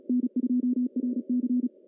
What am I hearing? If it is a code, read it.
Morse code, heard as a keyed tone, about 36 wpm, 250 Hz. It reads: N2RC
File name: CW / cqww-cw-k5tr